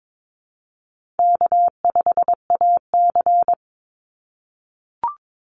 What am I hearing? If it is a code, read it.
Morse code, heard as a keyed tone, about 22 wpm, 700 Hz. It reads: K5AC
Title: USA Call Signs 22wpm